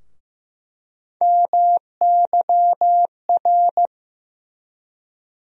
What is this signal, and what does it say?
Morse code, heard as a keyed tone, about 15 wpm, 700 Hz. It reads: MYR
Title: USA Call Sign Postfixes - 3 Letters 15wpm